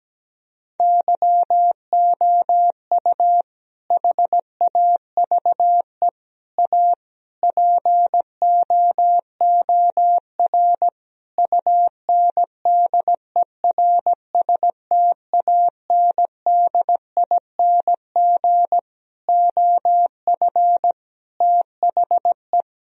Morse code, heard as a keyed tone, about 17 wpm, 700 Hz. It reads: YOU HAVE A POOR UNDERSTANDING OF THE
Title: Sentences from Top 1000 Words - Review Part 3 - 17wpm